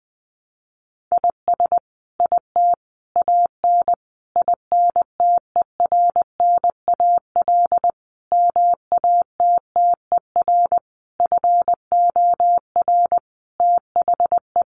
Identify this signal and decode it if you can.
Morse code, heard as a keyed tone, about 20 wpm, 700 Hz. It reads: IS IT AN INTERNAL MATTER FOR THE
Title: Sentences from Top 1400 Words 20wpm